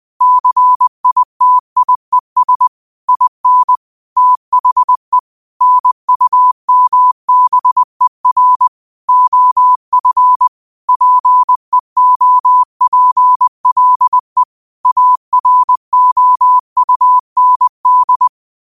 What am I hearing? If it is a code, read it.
Morse code, heard as a keyed tone, about 20 wpm, 1000 Hz. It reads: CITIES IN THE NUMBER OF PEOPLE AROUND
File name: qotdfiles / 20wpm